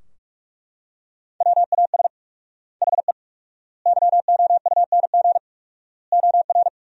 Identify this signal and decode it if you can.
Morse code, heard as a keyed tone, about 42 wpm, 700 Hz. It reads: WAS HE YOUNG OR
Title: 4x Word-Spacing - Sentences from Top 400 Words - 45wpm